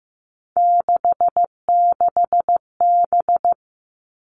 Morse code, heard as a keyed tone, about 15 wpm, 700 Hz. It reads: 66B
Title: All Cut Numbers - 3-Digit 15wpm